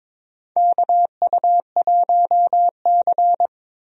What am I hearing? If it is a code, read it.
Morse code, heard as a keyed tone, about 22 wpm, 700 Hz. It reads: KU1C
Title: USA Call Signs - 4 Characters - 22wpm